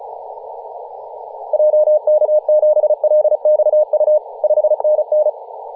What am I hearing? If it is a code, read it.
Morse code, heard as a keyed tone, about 34 wpm, 610 Hz, with radio noise. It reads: JK7LXU 5NN
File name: posts / 2017-nov-04-2233z-vk9ma-21-025mhz-cw